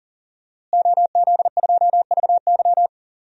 Morse code, heard as a keyed tone, about 40 wpm, 700 Hz. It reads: OZ2VY
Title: DXCC-100 Call Signs - 5 Characters - Repeated 3x - 40wpm